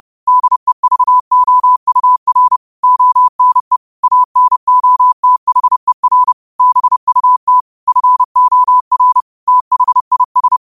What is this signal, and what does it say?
Morse code, heard as a keyed tone, about 30 wpm, 1000 Hz. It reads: DEVOUR ONE ANOTHER BUT FOR THIS